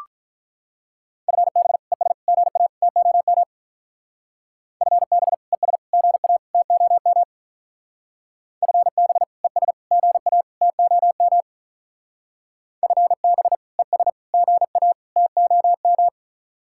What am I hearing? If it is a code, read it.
Morse code, heard as a keyed tone, about 42 wpm, 700 Hz. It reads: FB ES GA TOM FB ES GA TOM FB ES GA TOM FB ES GA TOM
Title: Speed-Racing Common QSO Phrases - 35wpm